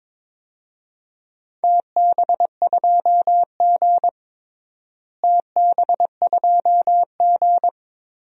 Morse code, heard as a keyed tone, about 21 wpm, 700 Hz. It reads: TB2G TB2G
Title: DXCC-100 Call Signs - 4 Characters - Repeated 3x - 22wpm